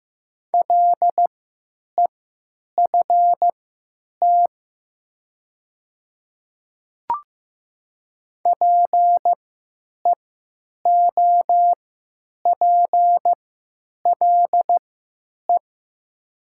Morse code, heard as a keyed tone, about 16 wpm, spaced as for 9 wpm, 700 Hz. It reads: LEFT PEOPLE
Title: 3x Character Spacing - Top 300 Words - 15wpm